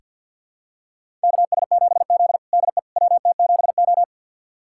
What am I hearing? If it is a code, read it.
Morse code, heard as a keyed tone, about 48 wpm, 700 Hz. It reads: KS7Z DE WT7O
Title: USA Callsign DE Callsign - 4 Characters - 50wpm